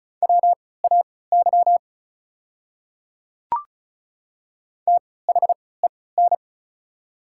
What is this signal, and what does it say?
Morse code, heard as a keyed tone, about 36 wpm, spaced as for 20 wpm, 690 Hz. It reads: WAY THEN
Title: 3x Character Spacing - Top 500 Words - 35wpm